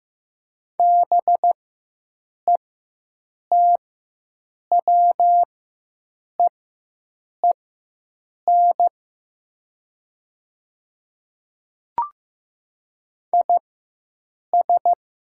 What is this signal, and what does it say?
Morse code, heard as a keyed tone, about 15 wpm, spaced as for 7 wpm, 700 Hz. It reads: BETWEEN IS
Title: 4x Character Spacing - Top 300 Words - 15wpm